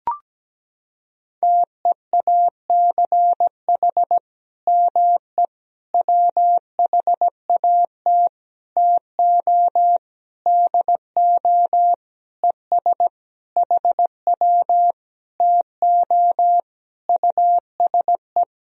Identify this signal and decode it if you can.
Morse code, heard as a keyed tone, about 17 wpm, 700 Hz. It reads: TEACH ME WHAT TO DO ES HW TO USE
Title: Sentences from Top 500 Words and Abbreviations 17wpm